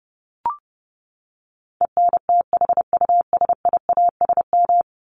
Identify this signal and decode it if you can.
Morse code, heard as a keyed tone, about 30 wpm, 700 Hz. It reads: ENTHUSIASM
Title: Top 1751-2000 Words 30wpm